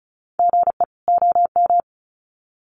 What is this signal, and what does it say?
Morse code, heard as a keyed tone, about 34 wpm, 700 Hz. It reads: GE OM
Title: Common QSO Phrases 35wpm